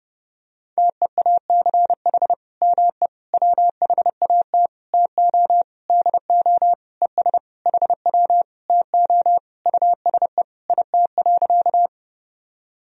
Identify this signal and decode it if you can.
Morse code, heard as a keyed tone, about 30 wpm, 700 Hz. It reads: TEACH ME WHAT TO DO ES HW TO USE IT.
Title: Sentences from Top 500 Words and Abbreviations 30wpm